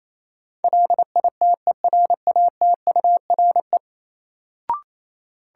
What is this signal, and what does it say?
Morse code, heard as a keyed tone, about 28 wpm, 700 Hz. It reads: LITERATURE
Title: Sentences from Top 600 Words - New Word Drill - 28wpm